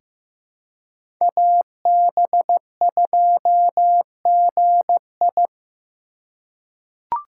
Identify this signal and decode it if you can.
Morse code, heard as a keyed tone, about 15 wpm, 700 Hz. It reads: AB2GI
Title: K1USN SST QSO Practice - 15wpm